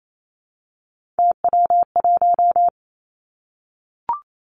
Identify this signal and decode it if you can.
Morse code, heard as a keyed tone, about 28 wpm, 700 Hz. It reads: TW1
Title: Three Challenging Letter-Number Combinations 28wpm